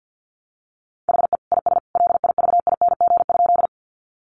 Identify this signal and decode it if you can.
Morse code, heard as a keyed tone, about 50 wpm, 700 Hz. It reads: HE IS LIVING?
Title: Sentences from Top 1100 Words 50wpm